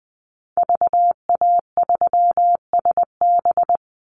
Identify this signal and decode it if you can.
Morse code, heard as a keyed tone, about 20 wpm, 700 Hz. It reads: VA3SB
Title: Call Signs Repeated 2x - 20wpm